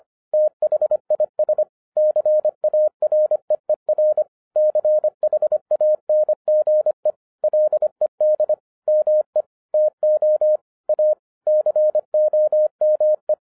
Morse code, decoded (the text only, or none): THIS CAREER CHANGE LED ME TO A COME